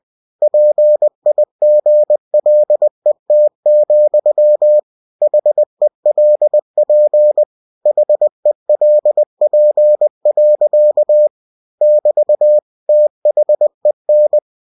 Morse code, heard as a keyed tone, about 20 wpm, 600 Hz. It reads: PIGLET, HELP HELP. <BT> THEN
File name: cw-books / winnie-the-pooh / winnie-20wpm-page-0092